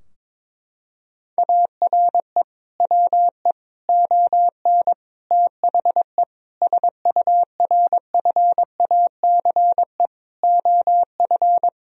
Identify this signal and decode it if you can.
Morse code, heard as a keyed tone, about 22 wpm, 700 Hz. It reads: ARE WE ON THE SURFACE OF
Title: Sentences from Top 1000 Words - Review Part 3 - 22wpm